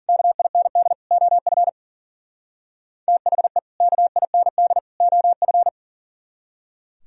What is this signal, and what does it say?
Morse code, heard as a keyed tone, about 50 wpm, 700 Hz. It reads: KIND OF THE KIND OF